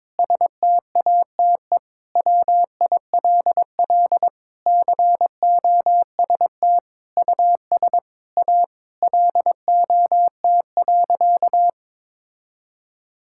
Morse code, heard as a keyed tone, about 21 wpm, 700 Hz. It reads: STATE WILL COST US A LOT.
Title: Sentences from Top 1000 Words - Review Part 5 - 22wpm